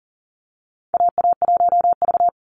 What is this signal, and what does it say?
Morse code, heard as a keyed tone, about 40 wpm, 700 Hz. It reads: AA1V
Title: Call Signs - 40wpm